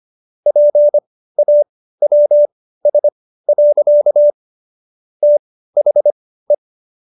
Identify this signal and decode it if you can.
Morse code, heard as a keyed tone, about 25 wpm, spaced as for 15 wpm, 590 Hz. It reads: PAWS. THE